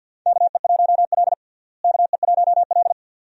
Morse code, heard as a keyed tone, about 53 wpm, 690 Hz. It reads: KE1L KE1L
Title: USA Call Signs - 4 Characters - Repeated 2x - 50wpm